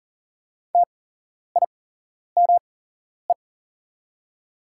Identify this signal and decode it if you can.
Morse code, heard as a keyed tone, about 40 wpm, spaced as for 11 wpm, 690 Hz. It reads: TIME
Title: 8x Character Spacing - Top 100 Words - 40wpm